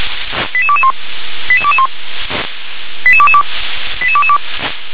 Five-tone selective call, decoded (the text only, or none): 88221 88221 88222 88222